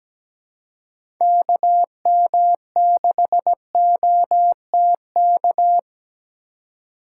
Morse code, heard as a keyed tone, about 17 wpm, 700 Hz. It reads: KM6OTK
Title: USA Call Signs - 6 Characters - 4x Character Spacing - 17wpm